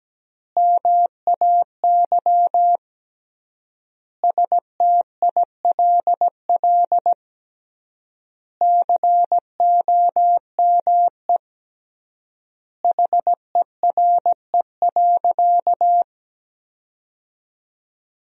Morse code, heard as a keyed tone, about 17 wpm, 700 Hz. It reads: MAY STILL COME HERE.
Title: 3x Word-Spacing - Sentences from Top 300 Words - 17wpm